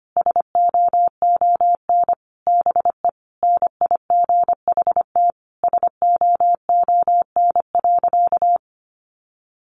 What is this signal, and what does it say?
Morse code, heard as a keyed tone, about 25 wpm, 700 Hz. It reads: SOON BE NIGHT SOON.
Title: Sentences from Top 1300 Words 25wpm